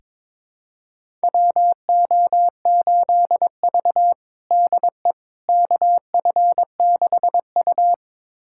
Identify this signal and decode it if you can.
Morse code, heard as a keyed tone, about 22 wpm, 700 Hz. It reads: WO8V DE KF6U
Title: USA Callsign DE Callsign - 4 Characters - 22wpm